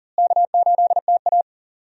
Morse code, heard as a keyed tone, about 40 wpm, 700 Hz. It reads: K8TA
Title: USA Call Signs - 4 Characters - Repeated 2x - 40wpm